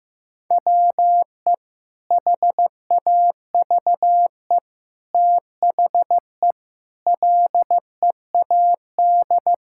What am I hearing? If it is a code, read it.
Morse code, heard as a keyed tone, about 15 wpm, 700 Hz. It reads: WE HAVE THE LEAD
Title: Sentences from Top 1000 Words - Review Part 5 - 15wpm